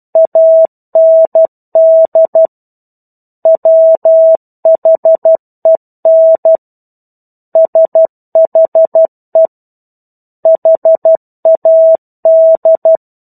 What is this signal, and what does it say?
Morse code, heard as a keyed tone, about 12 wpm, 650 Hz. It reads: AND WHEN SHE HAD
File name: 01-Chapter-01 12 WPM